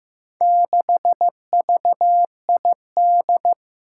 Morse code, heard as a keyed tone, about 15 wpm, 700 Hz. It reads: 6VID
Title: Four Challenging Letter-Number Combinations 15wpm